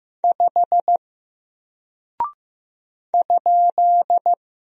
Morse code, heard as a keyed tone, about 15 wpm, 700 Hz. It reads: 5 ?
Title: Single Character (Letter:Number:Punctuation:Prosign) Alt 15wpm